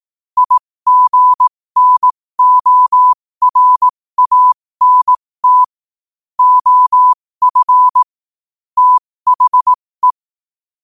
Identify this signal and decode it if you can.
Morse code, heard as a keyed tone, about 18 wpm, spaced as for 15 wpm, 1000 Hz. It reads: IGNORANT OF THE
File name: qotdfiles / 15wpm